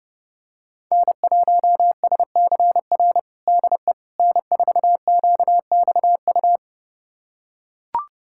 Morse code, heard as a keyed tone, about 30 wpm, 700 Hz. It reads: N1SCR DE N4QXU
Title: USA Callsign DE Callsign - 5 Characters - 30wpm